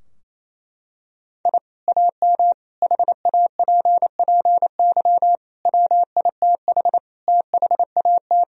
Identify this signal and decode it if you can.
Morse code, heard as a keyed tone, about 28 wpm, 700 Hz. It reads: I AM HAPPY WITH THAT
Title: Sentences from Top 1000 Words - Review Part 2 - 28wpm